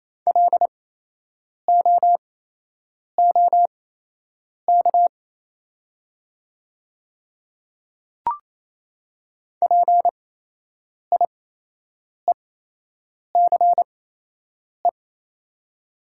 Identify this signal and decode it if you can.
Morse code, heard as a keyed tone, about 28 wpm, spaced as for 8 wpm, 700 Hz. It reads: LOOK PIECE
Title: 8x Character Spacing - Top 500 Words - 28wpm